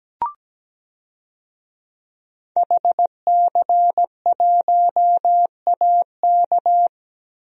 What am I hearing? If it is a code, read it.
Morse code, heard as a keyed tone, about 17 wpm, 700 Hz. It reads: HC1AK
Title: DXCC-100 Call Signs - 5 Characters - Repeated 2x - 17wpm